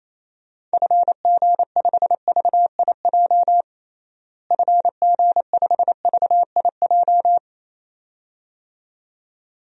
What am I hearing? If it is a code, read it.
Morse code, heard as a keyed tone, about 28 wpm, 700 Hz. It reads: FG5VIJ FG5VIJ
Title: DXCC-100 Call Signs - 6 Characters - Repeated 2x - 28wpm